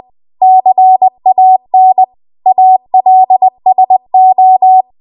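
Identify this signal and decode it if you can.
Morse code, heard as a keyed tone, about 20 wpm, 750 Hz. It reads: CAN ALSO
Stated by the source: Morse code practice at 20 WPM.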